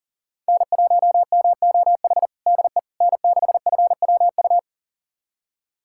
Morse code, heard as a keyed tone, about 40 wpm, 700 Hz. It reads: N1MOH DE N6FWU
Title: USA Callsign DE Callsign - 5 Characters - 40wpm